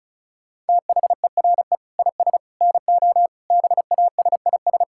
Morse code, heard as a keyed tone, about 36 wpm, 700 Hz. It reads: THERE IS NO BASIS
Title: Sentences from Top 1000 Words - Review Part 5 - 35wpm